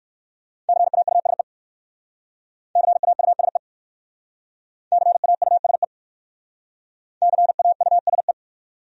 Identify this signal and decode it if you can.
Morse code, heard as a keyed tone, about 54 wpm, 700 Hz. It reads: CAUSE CAUSE CAUSE CAUSE
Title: Speed-Racing Top 500 Words 45wpm